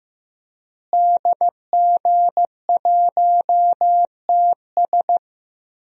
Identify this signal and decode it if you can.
Morse code, heard as a keyed tone, about 15 wpm, 700 Hz. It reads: DG1TS
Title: DXCC-100 Call Signs - 5 Characters - Repeated 2x - 15wpm